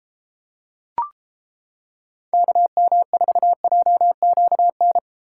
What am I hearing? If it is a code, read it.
Morse code, heard as a keyed tone, about 29 wpm, 700 Hz. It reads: KM4JQN